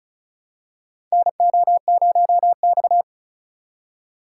Morse code, heard as a keyed tone, about 36 wpm, 700 Hz. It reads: NO0X
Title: USA Call Signs - 4 Characters - 2x Character Spacing - 35wpm